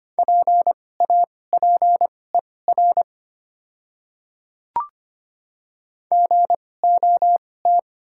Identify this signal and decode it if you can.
Morse code, heard as a keyed tone, about 25 wpm, spaced as for 18 wpm, 700 Hz. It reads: PAPER GOT
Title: 2x Character Spacing - Top 300 Words - 25wpm